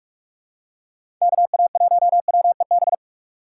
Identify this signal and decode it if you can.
Morse code, heard as a keyed tone, about 44 wpm, 700 Hz. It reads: KA1WEB
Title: USA Call Signs - 6 Characters - 6x Character Spacing - 45wpm